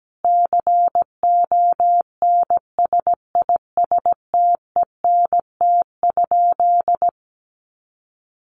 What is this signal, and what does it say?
Morse code, heard as a keyed tone, about 17 wpm, 700 Hz. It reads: CONSISTENT?
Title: Sentences from Top 1500 Words 17wpm